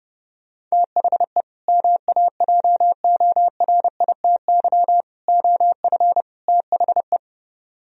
Morse code, heard as a keyed tone, about 34 wpm, 690 Hz. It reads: THE MAJORITY OF THE